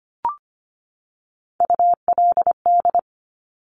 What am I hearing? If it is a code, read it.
Morse code, heard as a keyed tone, about 25 wpm, 700 Hz. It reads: ULD
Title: Top Three Letter Combinations in Words 25wpm